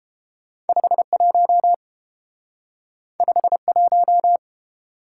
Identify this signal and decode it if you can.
Morse code, heard as a keyed tone, about 28 wpm, 700 Hz. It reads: 51 51